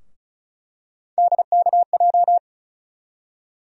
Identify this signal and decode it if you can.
Morse code, heard as a keyed tone, about 36 wpm, 700 Hz. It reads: DKJ
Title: USA Call Sign Postfixes - 3 Letters 35wpm